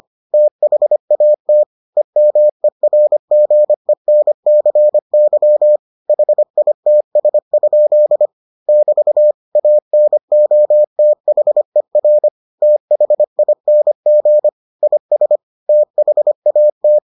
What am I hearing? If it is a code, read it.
Morse code, heard as a keyed tone, about 25 wpm, 600 Hz. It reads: THAT EMERGENCY HITS? <BT> ANOTHER THING IS THAT